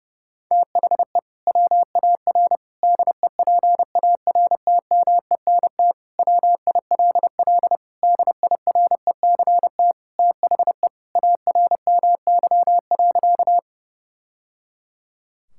Morse code, heard as a keyed tone, about 32 wpm, 700 Hz. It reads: THE WAR DEPARTMENT WILL DIRECT THE ARMY.
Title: Sentences from Top 1000 Words - Review Part 5 - 30wpm